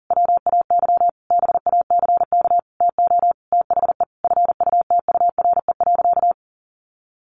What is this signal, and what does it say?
Morse code, heard as a keyed tone, about 40 wpm, 700 Hz. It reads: WAY BACK TO THE FUTURE.
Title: A Sound Of Thunder 40wpm